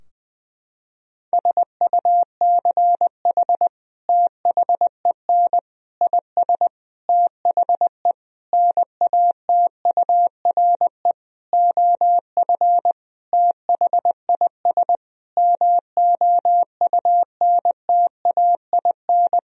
Morse code, heard as a keyed tone, about 20 wpm, 700 Hz. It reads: SUCH THEN IS THE NATURE OF THIS MOUNTAIN
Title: Sentences from Top 1000 Words - Review Part 4 - 20wpm